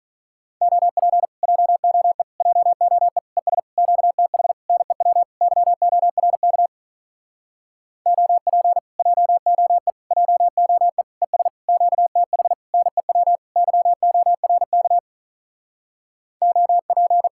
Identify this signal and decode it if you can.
Morse code, heard as a keyed tone, about 46 wpm, 700 Hz. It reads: OP JOE JOE ES QTH NEW YORK OP JOE JOE ES QTH NEW YORK OP